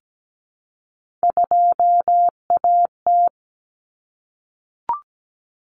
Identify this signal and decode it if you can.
Morse code, heard as a keyed tone, about 17 wpm, 700 Hz. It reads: 2AT
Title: Common Cut Numbers - 3-Digit 17wpm